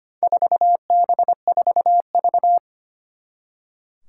Morse code, heard as a keyed tone, about 25 wpm, 700 Hz. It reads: 4B4V
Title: Four Challenging Letter-Number Combinations 25wpm